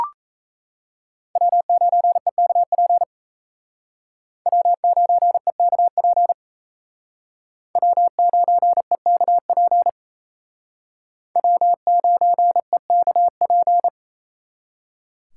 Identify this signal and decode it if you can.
Morse code, heard as a keyed tone, about 34 wpm, 700 Hz. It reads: W9EKP W9EKP W9EKP W9EKP
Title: Speed Racing - USA 1x3 Call Signs 28wpm